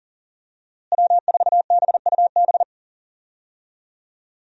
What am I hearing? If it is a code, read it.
Morse code, heard as a keyed tone, about 40 wpm, 700 Hz. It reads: W4BUB
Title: USA Call Signs - 5 Characters - Repeated 3x - 40wpm